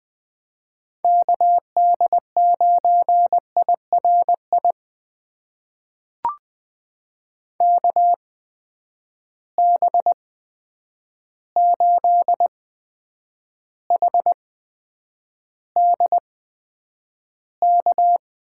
Morse code, heard as a keyed tone, about 20 wpm, spaced as for 9 wpm, 700 Hz. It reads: KD9IRI KB8HDK
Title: USA Call Signs - 6 Characters - 8x Character Spacing - 20wpm